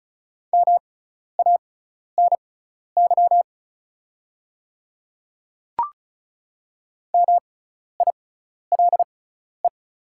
Morse code, heard as a keyed tone, about 36 wpm, spaced as for 12 wpm, 690 Hz. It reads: MANY MILE
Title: 6x Character Spacing - Top 300 Words - 35wpm